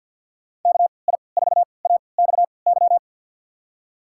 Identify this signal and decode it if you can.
Morse code, heard as a keyed tone, about 52 wpm, spaced as for 29 wpm, 700 Hz. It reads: KI4AXY
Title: USA Call Signs - 6 Characters - 3x Character Spacing - 50wpm